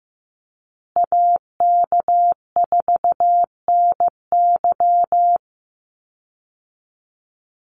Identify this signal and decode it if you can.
Morse code, heard as a keyed tone, about 15 wpm, 700 Hz. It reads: AK4NY
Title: Call Signs Repeated 2x - 15wpm